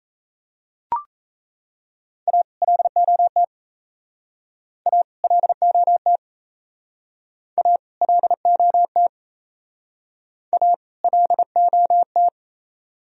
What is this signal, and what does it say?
Morse code, heard as a keyed tone, about 36 wpm, 700 Hz. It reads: A LOT A LOT A LOT A LOT